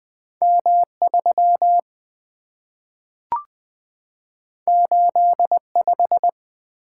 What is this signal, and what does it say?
Morse code, heard as a keyed tone, about 20 wpm, 700 Hz. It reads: M 3 8 5
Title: Two Letter-Number Combinations - 20wpm